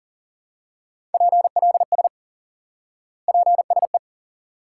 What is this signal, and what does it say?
Morse code, heard as a keyed tone, about 40 wpm, 700 Hz. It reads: PLS PSE
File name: IQE Course 3 - Lesson 89 - YAGI - Two Elements - 40wpm